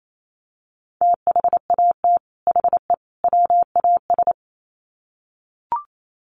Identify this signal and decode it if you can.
Morse code, heard as a keyed tone, about 28 wpm, 700 Hz. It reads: THAT HE WAS
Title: Sets of 3 Words 28wpm